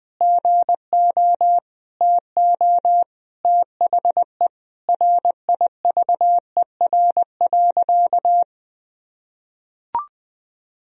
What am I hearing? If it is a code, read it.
Morse code, heard as a keyed tone, about 20 wpm, 700 Hz. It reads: GO TO THE RIVER.
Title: Sentences from Top 1000 Words - Review Part 4 - 20wpm